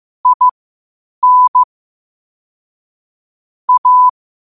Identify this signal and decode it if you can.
Morse code, heard as a keyed tone, about 15 wpm, spaced as for 8 wpm, 1000 Hz. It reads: IN A
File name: qotdfiles / 8wpm